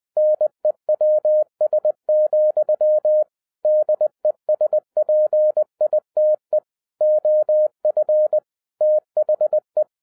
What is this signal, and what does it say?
Morse code, heard as a keyed tone, about 20 wpm, 610 Hz. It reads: NEWS, DESPITE OF THE